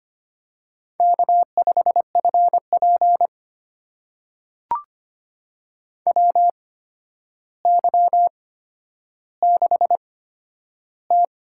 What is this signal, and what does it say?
Morse code, heard as a keyed tone, about 25 wpm, spaced as for 11 wpm, 700 Hz. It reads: K5FP WY6T
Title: USA Call Signs - 4 Characters - 8x Character Spacing - 25wpm